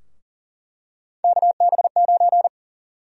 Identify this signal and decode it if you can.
Morse code, heard as a keyed tone, about 40 wpm, 700 Hz. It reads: KB9
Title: USA Call Sign Prefixes - 40wpm